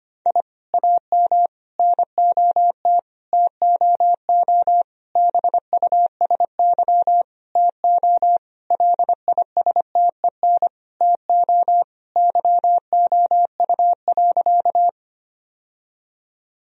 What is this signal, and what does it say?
Morse code, heard as a keyed tone, about 25 wpm, 700 Hz. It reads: I AM NOT TOO BUSY TO LISTEN TO YOU.
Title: Sentences from Top 1000 Words - Review Part 3 - 25wpm